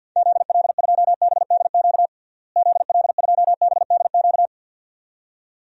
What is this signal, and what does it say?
Morse code, heard as a keyed tone, about 53 wpm, 700 Hz. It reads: ZL2BDQ ZL2BDQ
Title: DXCC-100 Call Signs - 6 Characters - Repeated 2x - 50wpm